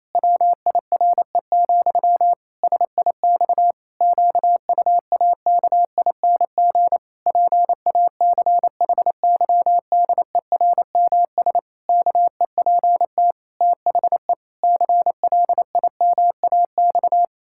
Morse code, heard as a keyed tone, about 27 wpm, 700 Hz. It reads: WIRE, SIX QUAKING PACHYDERMS KEPT THE CLIMAX
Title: Pangram Challenge v2 - 28wpm